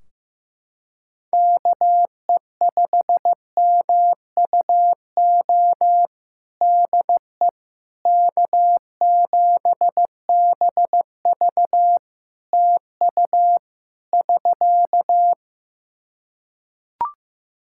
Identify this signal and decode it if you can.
Morse code, heard as a keyed tone, about 15 wpm, 700 Hz. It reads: KE5MUO DE K7BV TU <SK>
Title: K1USN SST QSO Practice 15wpm